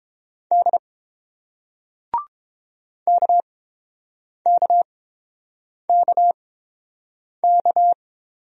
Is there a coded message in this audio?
Yes: Morse code from 0.5 to 7.9 s, heard as a keyed tone, about 30 wpm, spaced as for 7 wpm, 700 Hz. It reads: D KKKK